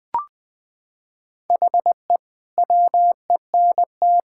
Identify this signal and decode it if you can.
Morse code, heard as a keyed tone, about 21 wpm, 700 Hz. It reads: HE WENT
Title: Sentences from Top 1000 Words - Review Part 5 - 20wpm